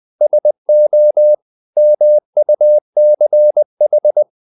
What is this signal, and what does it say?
Morse code, heard as a keyed tone, about 20 wpm, 600 Hz. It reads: SO MUCH
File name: the old man and the sea / the old man and the sea part 1 20wpm